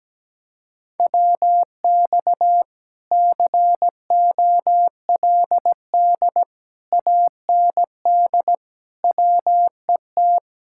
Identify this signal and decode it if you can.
Morse code, heard as a keyed tone, about 17 wpm, 700 Hz. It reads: WX COLD AND WET
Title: Speed-Racing Common QSO Phrases Encore - 17wpm